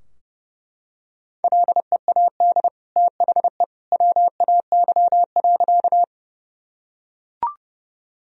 Morse code, heard as a keyed tone, about 30 wpm, 700 Hz. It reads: LEAD THE WAY.
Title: Sentences from Top 1000 Words - Review Part 2 - 30wpm